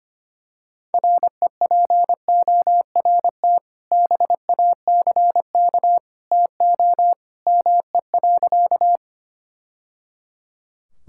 Morse code, heard as a keyed tone, about 25 wpm, 700 Hz. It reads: REPORT BACK TO ME.
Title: Sentences from Top 1000 Words - Review Part 6 - 25wpm